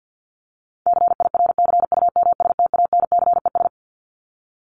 Morse code, heard as a keyed tone, about 48 wpm, 700 Hz. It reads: CIRCUMSTANCES
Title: ES:S Suffix - Top 100 Words with Base Words 50wpm